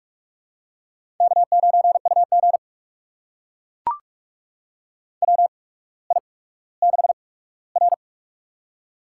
Morse code, heard as a keyed tone, about 44 wpm, spaced as for 19 wpm, 700 Hz. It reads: K9UG WI6R
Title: USA Call Signs - 4 Characters - 8x Character Spacing - 45wpm